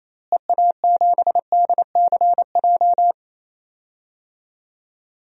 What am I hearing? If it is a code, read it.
Morse code, heard as a keyed tone, about 28 wpm, 700 Hz. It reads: EA7DCJ
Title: DXCC-100 Call Signs - 6 Characters - Repeated 2x - 28wpm